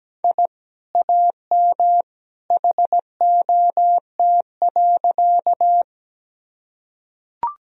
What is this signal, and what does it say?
Morse code, heard as a keyed tone, about 17 wpm, 700 Hz. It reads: I AM HOT.
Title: Sentences from Top 1000 Words - Review Part 3 - 17wpm